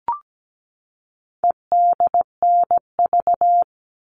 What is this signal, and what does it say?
Morse code, heard as a keyed tone, about 17 wpm, 700 Hz. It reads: EDNV
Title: All Cut Numbers - 4-Digit 17wpm